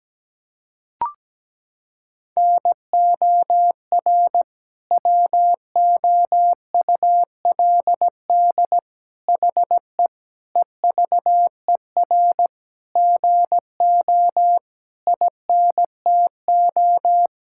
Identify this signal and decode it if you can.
Morse code, heard as a keyed tone, about 17 wpm, 700 Hz. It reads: NOR WOULD HE EVER GO INTO
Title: Sentences from Top 1600 Words 17wpm